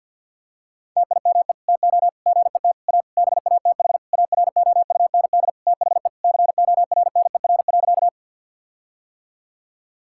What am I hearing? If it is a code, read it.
Morse code, heard as a keyed tone, about 49 wpm, 700 Hz. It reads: TIME TO GET A BATH AROUND THE CORNER.
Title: Sentences from Top 1600 Words 50wpm